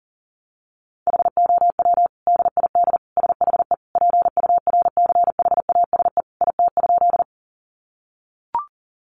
Morse code, heard as a keyed tone, about 40 wpm, 700 Hz. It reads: HOW DID SHE PURCHASE IT?
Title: Sentences from Top 800 Words 40wpm